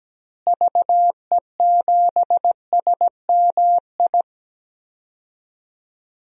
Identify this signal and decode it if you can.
Morse code, heard as a keyed tone, about 17 wpm, 700 Hz. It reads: VE7SMI
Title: DXCC-100 Call Signs - 6 Characters - Repeated 2x - 17wpm